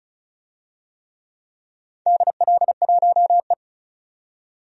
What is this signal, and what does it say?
Morse code, heard as a keyed tone, about 36 wpm, 700 Hz. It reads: DL1E
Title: DXCC-100 Call Signs - 4 Characters - 35wpm